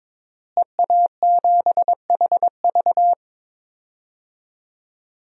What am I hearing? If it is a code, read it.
Morse code, heard as a keyed tone, about 22 wpm, 700 Hz. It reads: EA7HV
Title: DXCC-100 Call Signs - 5 Characters - Repeated 2x - 22wpm